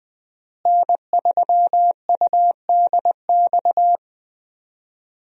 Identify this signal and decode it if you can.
Morse code, heard as a keyed tone, about 20 wpm, 700 Hz. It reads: N3UDX
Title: USA Call Signs - 5 Characters - Repeated 3x - 20wpm